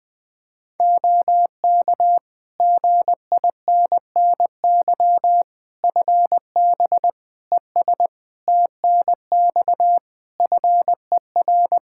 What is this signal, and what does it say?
Morse code, heard as a keyed tone, about 20 wpm, 700 Hz. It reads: OK GINNY FB ES TNX FER
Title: Practice QSO - Exchange 2 - Info - 20wpm